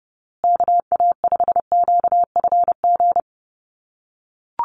Morse code, heard as a keyed tone, about 30 wpm, 700 Hz. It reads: KA5QFG
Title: USA Call Signs Repeated 3x 30wpm